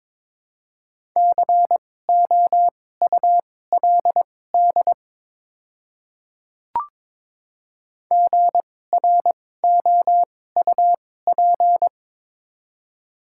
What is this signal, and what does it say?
Morse code, heard as a keyed tone, about 22 wpm, spaced as for 16 wpm, 700 Hz. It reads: COULD GROUP
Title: 2x Character Spacing - Top 300 Words - 22wpm